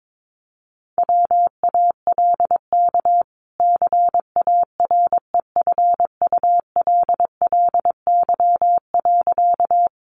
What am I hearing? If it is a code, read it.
Morse code, heard as a keyed tone, about 22 wpm, 700 Hz. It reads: WALK CAREFULLY.
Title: Sentences from Top 1300 Words 22wpm